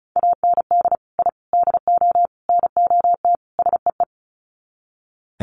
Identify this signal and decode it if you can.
Morse code, heard as a keyed tone, about 34 wpm, 700 Hz. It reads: AND I DO NOT SEE
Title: Sets of 5 Words Encore2 35wpm